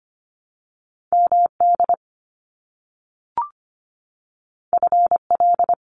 Morse code, heard as a keyed tone, about 25 wpm, 700 Hz. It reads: MD FL
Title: US States 25wpm